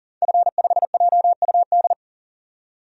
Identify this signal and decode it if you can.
Morse code, heard as a keyed tone, about 39 wpm, 700 Hz. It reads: F5JUD
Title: DXCC-100 Call Signs - 5 Characters - 40wpm